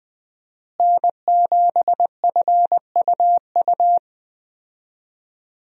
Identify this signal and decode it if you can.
Morse code, heard as a keyed tone, about 20 wpm, 700 Hz. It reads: N7FUU
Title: USA Call Signs - 5 Characters - Repeated 3x - 20wpm